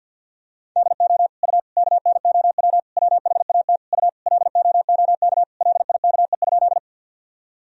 Morse code, heard as a keyed tone, about 49 wpm, 700 Hz. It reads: DO U KNOW WHAT U LOOK LIKE?
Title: Sentences from Top 100 Words and Abbreviations 50wpm